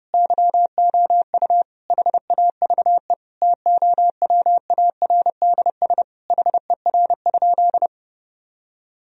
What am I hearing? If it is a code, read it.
Morse code, heard as a keyed tone, about 32 wpm, 700 Hz. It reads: YOU HAVE TOWARDS HER?
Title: Sentences from Top 1000 Words - Review Part 4 - 30wpm